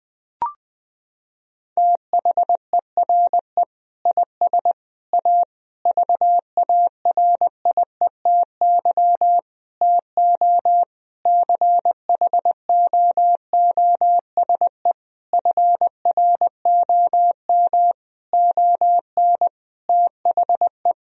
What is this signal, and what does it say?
Morse code, heard as a keyed tone, about 20 wpm, 700 Hz. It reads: THERE IS A VARIETY TO CHOOSE FROM ON THE
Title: Sentences from Top 1000 Words - Review Part 4 - 20wpm